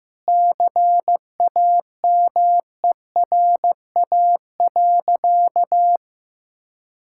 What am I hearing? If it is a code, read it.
Morse code, heard as a keyed tone, about 15 wpm, 700 Hz. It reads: CAMERA.
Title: Sentences from Top 1000 Words - Review Part 5 - 15wpm